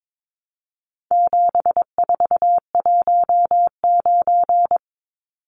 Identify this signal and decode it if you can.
Morse code, heard as a keyed tone, about 22 wpm, 700 Hz. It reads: 7419
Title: Numbers - 4-Digit 22wpm